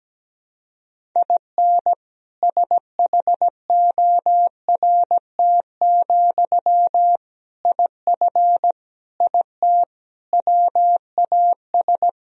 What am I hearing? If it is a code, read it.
Morse code, heard as a keyed tone, about 17 wpm, 700 Hz. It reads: IN SHORT, IF IT WAS
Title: Sentences from Top 1000 Words - Review Part 2 - 17wpm